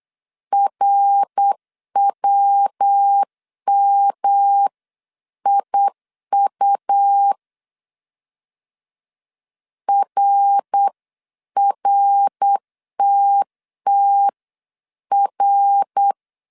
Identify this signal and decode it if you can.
Morse code, heard as a keyed tone, about 8 wpm, 800 Hz. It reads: RWMIU RRTTR